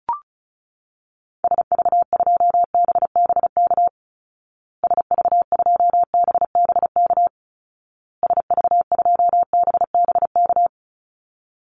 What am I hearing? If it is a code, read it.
Morse code, heard as a keyed tone, about 34 wpm, 700 Hz. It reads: SV2BBK SV2BBK SV2BBK
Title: Call Signs Repeated 2x - 35wpm